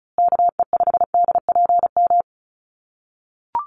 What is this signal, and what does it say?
Morse code, heard as a keyed tone, about 34 wpm, 700 Hz. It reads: KE5DPM
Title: USA Call Signs Repeated 2x 35wpm